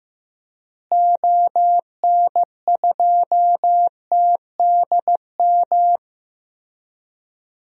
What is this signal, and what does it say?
Morse code, heard as a keyed tone, about 15 wpm, 700 Hz. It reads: ON2TDM
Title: DXCC-100 Call Signs - 6 Characters - Repeated 2x - 15wpm